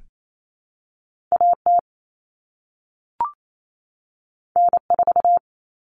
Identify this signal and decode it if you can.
Morse code, heard as a keyed tone, about 28 wpm, 700 Hz. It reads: AT N4
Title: All Cut Numbers - 2-Digit 28wpm